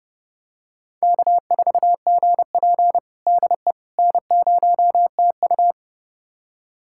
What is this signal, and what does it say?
Morse code, heard as a keyed tone, about 30 wpm, 700 Hz. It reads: K4GP DE N0TU
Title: USA Callsign DE Callsign - 4 Characters - 30wpm